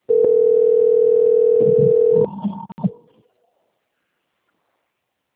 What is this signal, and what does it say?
Ringback tone (North America), 440 + 480 Hz.